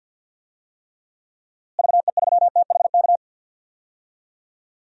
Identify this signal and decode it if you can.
Morse code, heard as a keyed tone, about 46 wpm, 700 Hz. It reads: VE3THK
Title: DXCC-100 Call Signs - 6 Characters - Repeated 3x - 50wpm